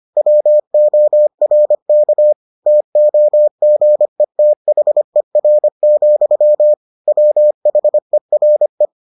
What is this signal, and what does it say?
Morse code, heard as a keyed tone, about 25 wpm, 600 Hz. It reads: WORK TOGETHER, WHERE